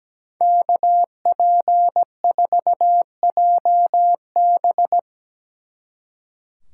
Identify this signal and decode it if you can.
Morse code, heard as a keyed tone, about 19 wpm, 700 Hz. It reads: KP4JB
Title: Speed Racing - USA 2x2 Call Signs 17wpm